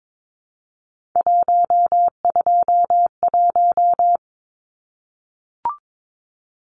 Morse code, heard as a keyed tone, about 22 wpm, 700 Hz. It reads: 121
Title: Numbers - 3-Digit 22wpm